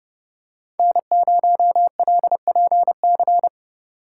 Morse code, heard as a keyed tone, about 29 wpm, 700 Hz. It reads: N0LPC